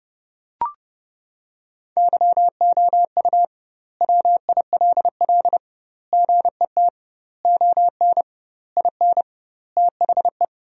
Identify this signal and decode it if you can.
Morse code, heard as a keyed tone, about 32 wpm, 700 Hz. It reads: YOU WILL GET ON IN THE
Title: 2x Word-Spacing - Sentences from Top 200 Words - 30wpm